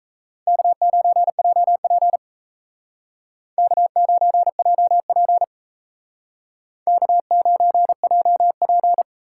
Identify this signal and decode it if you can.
Morse code, heard as a keyed tone, about 35 wpm, 700 Hz. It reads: K9JP K9JP K9JP
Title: Speed Racing - USA Call Signs 28wpm